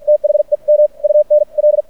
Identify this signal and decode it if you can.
Morse code, heard as a keyed tone, about 50 wpm, spaced as for 43 wpm, 600 Hz. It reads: THEM UN L
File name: archived / CRSnet